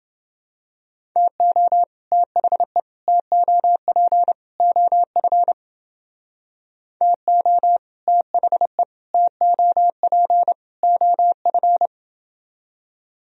Morse code, heard as a keyed tone, about 28 wpm, 700 Hz. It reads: TO THE TOP OF TO THE TOP OF